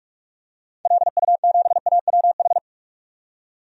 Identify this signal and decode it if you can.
Morse code, heard as a keyed tone, about 46 wpm, 700 Hz. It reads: LU7AWH
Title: DXCC-100 Call Signs - 6 Characters - 45wpm